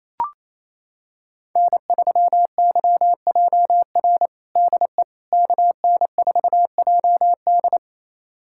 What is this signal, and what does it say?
Morse code, heard as a keyed tone, about 28 wpm, 700 Hz. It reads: N3YJR DE KN4JD
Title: USA Callsign DE Callsign - 5 Characters - 28wpm